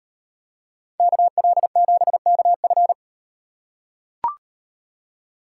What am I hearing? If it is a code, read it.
Morse code, heard as a keyed tone, about 38 wpm, 700 Hz. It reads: KL7KF